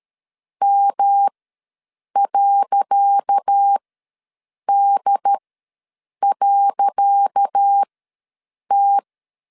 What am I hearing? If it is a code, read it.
Morse code, heard as a keyed tone, about 13 wpm, spaced as for 7 wpm, 800 Hz. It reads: M.D.T
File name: LEZIONI / 10 SIGNS / 10C DOT ws 12wpm 3548